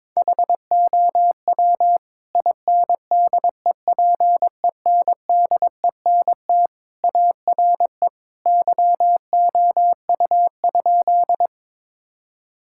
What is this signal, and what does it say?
Morse code, heard as a keyed tone, about 22 wpm, 700 Hz. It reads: HOW INDEPENDENT ARE YOU?
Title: Sentences from Top 1000 Words - Review Part 6 - 22wpm